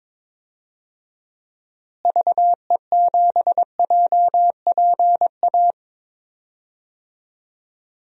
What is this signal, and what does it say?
Morse code, heard as a keyed tone, about 22 wpm, 700 Hz. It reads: VE7JPA
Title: DXCC-100 Call Signs - 6 Characters - Repeated 2x - 22wpm